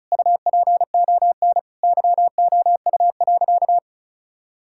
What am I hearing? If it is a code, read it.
Morse code, heard as a keyed tone, about 34 wpm, 700 Hz. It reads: UPON YOU.
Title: Sentences from Top 1600 Words 35wpm